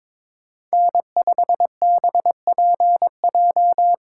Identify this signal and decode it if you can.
Morse code, heard as a keyed tone, about 22 wpm, 700 Hz. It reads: N5BPJ
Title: USA Call Signs - 5 Characters - 4x Character Spacing - 22wpm